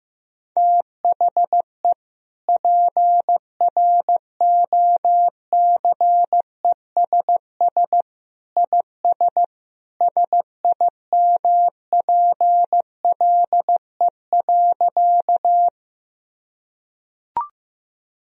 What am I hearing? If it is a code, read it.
Morse code, heard as a keyed tone, about 15 wpm, 700 Hz. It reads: THE PROCESS IS SIMPLE.
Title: Sentences from Top 1000 Words - Review Part 3 - 15wpm